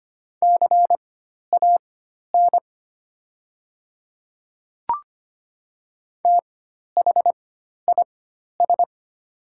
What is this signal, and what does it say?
Morse code, heard as a keyed tone, about 25 wpm, spaced as for 12 wpm, 700 Hz. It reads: CAN THIS
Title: 4x Character Spacing - Top 200 Words - 25wpm